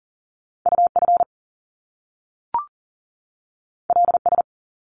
Morse code, heard as a keyed tone, about 40 wpm, 700 Hz. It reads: UF LS
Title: Top Two Letter Combinations in Words 40wpm